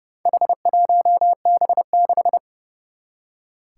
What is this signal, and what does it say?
Morse code, heard as a keyed tone, about 30 wpm, 700 Hz. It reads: H1B6
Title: Four Challenging Letter-Number Combinations 30wpm